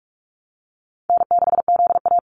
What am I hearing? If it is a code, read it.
Morse code, heard as a keyed tone, about 44 wpm, 700 Hz. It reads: N6ZA
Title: NA SOTA Activator Callsigns 45wpm